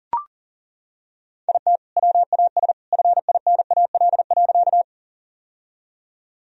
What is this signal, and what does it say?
Morse code, heard as a keyed tone, about 40 wpm, 700 Hz. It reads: IT WAS FINAL.
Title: Sentences from Top 1000 Words - Review Part 2 - 40wpm